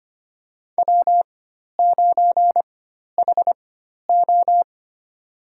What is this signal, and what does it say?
Morse code, heard as a keyed tone, about 25 wpm, spaced as for 12 wpm, 700 Hz. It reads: W9HO
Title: USA Call Signs - 4x Character Spacing - 25wpm